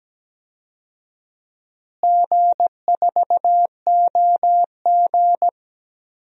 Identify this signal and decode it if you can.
Morse code, heard as a keyed tone, about 17 wpm, 700 Hz. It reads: G4OG
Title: DXCC-100 Call Signs - 4 Characters - Repeated 3x - 17wpm